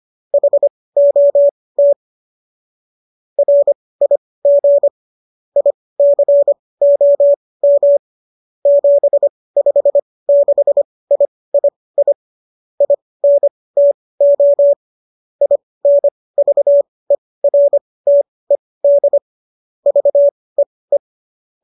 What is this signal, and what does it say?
Morse code, heard as a keyed tone, about 25 wpm, spaced as for 18 wpm, 570 Hz. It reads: HOT RIG ICOM 756III INTO INVERTED VEE
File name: qso223 18